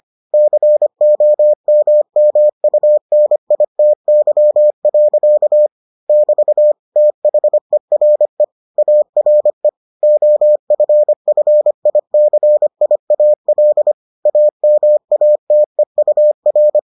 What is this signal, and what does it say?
Morse code, heard as a keyed tone, about 25 wpm, 600 Hz. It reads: COMMUNITY. <BT> THERE ARE OFFICIAL AMATEUR